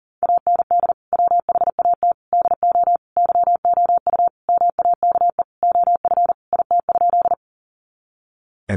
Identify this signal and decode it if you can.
Morse code, heard as a keyed tone, about 40 wpm, 700 Hz. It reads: AND WHAT DO YOU MAKE OF IT?
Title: Sentences from Top 100 Words 40wpm